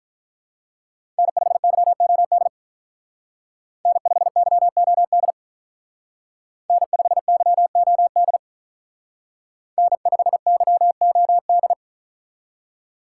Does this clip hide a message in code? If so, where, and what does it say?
Yes: Morse code from 1.2 to 11.7 s, heard as a keyed tone, about 45 wpm, 700 Hz. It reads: N5YOD N5YOD N5YOD N5YOD